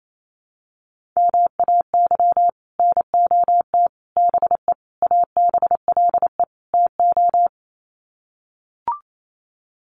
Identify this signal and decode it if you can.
Morse code, heard as a keyed tone, about 28 wpm, 700 Hz. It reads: MAY NOT BE ABLE TO
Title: Sets of 5 Words 28wpm